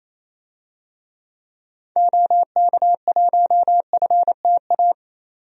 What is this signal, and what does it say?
Morse code, heard as a keyed tone, about 28 wpm, 700 Hz. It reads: OK1FTA
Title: DXCC-100 Call Signs - 6 Characters - 28wpm